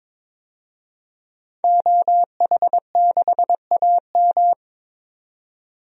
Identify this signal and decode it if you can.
Morse code, heard as a keyed tone, about 21 wpm, 700 Hz. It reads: OH6AM
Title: DXCC-100 Call Signs - 5 Characters - 22wpm